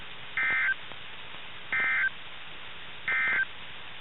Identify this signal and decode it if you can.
Emergency Alert System (SAME) tones, heard as harsh buzzing bursts. It is heard from 0.2 to 3.5 s.
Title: NOAA Weather Radio 162.4